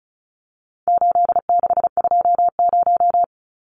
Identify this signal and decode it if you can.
Morse code, heard as a keyed tone, about 34 wpm, 700 Hz. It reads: 8620
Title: Numbers - 4-Digit 35wpm